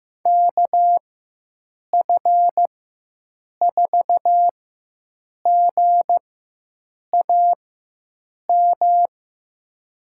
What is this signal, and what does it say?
Morse code, heard as a keyed tone, about 15 wpm, spaced as for 7 wpm, 700 Hz. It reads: KF4GAM
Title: USA Call Signs - 6 Characters - 4x Character Spacing - 15wpm